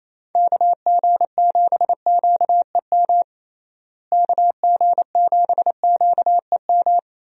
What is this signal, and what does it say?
Morse code, heard as a keyed tone, about 28 wpm, 700 Hz. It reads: KG7QEM KG7QEM
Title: USA Call Signs - 6 Characters - Repeated 3x - 28wpm